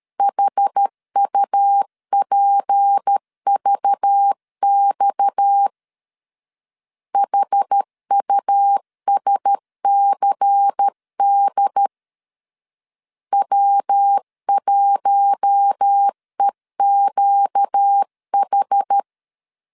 Morse code, heard as a keyed tone, about 13 wpm, 800 Hz. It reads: HUPVX HUSCD W1EQH
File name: LEZIONI / 7 GF9Z / 7 GF9Z 12wpm 2147